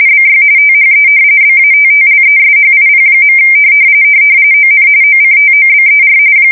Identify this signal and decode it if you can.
RTTY, about 45.45 baud. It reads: BY THE BATTLESHIP. ON OCTOBER
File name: fsk170 / battleshipnj